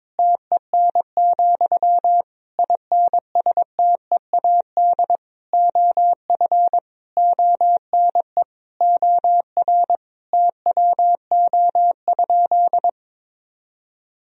Morse code, heard as a keyed tone, about 22 wpm, 700 Hz. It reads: TEN, INSTEAD OF ONE OR TWO?
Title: Sentences from Top 1000 Words - Review Part 4 - 22wpm